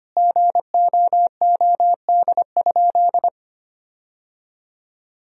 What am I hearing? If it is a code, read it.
Morse code, heard as a keyed tone, about 25 wpm, 700 Hz. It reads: GOOD?
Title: Sentences from Top 1000 Words - Review Part 3 - 25wpm